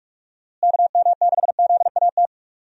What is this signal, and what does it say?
Morse code, heard as a keyed tone, about 46 wpm, 700 Hz. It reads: KM6ZAT
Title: USA Call Signs - 6 Characters - Repeated 2x - 45wpm